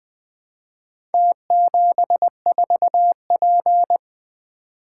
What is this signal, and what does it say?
Morse code, heard as a keyed tone, about 20 wpm, 700 Hz. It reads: T74P
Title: Four Letter-Number Combinations - 20wpm